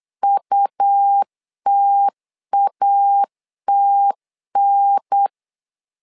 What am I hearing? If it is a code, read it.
Morse code, heard as a keyed tone, about 8 wpm, 800 Hz. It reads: UTATN
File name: LEZIONI / 3 MWRO / 3 M 8wpm 2300